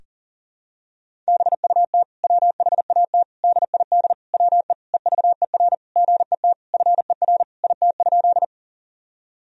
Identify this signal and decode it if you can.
Morse code, heard as a keyed tone, about 40 wpm, 700 Hz. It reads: BUT WHAT DID WE EVER GET FER IT?
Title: Sentences from Top 400 Words and Abbreviations 40wpm